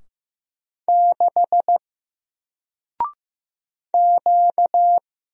Morse code, heard as a keyed tone, about 15 wpm, 700 Hz. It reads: 6 Q
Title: Single Letter-Number 15wpm